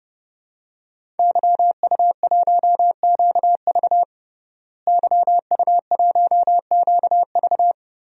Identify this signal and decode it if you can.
Morse code, heard as a keyed tone, about 30 wpm, 700 Hz. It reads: YU1QV YU1QV
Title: DXCC-100 Call Signs - 5 Characters - Repeated 2x - 30wpm